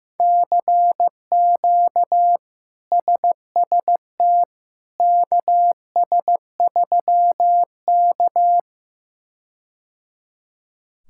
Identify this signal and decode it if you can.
Morse code, heard as a keyed tone, about 15 wpm, 700 Hz. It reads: CQ SST KS3K
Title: K1USN SST QSO Practice - 15wpm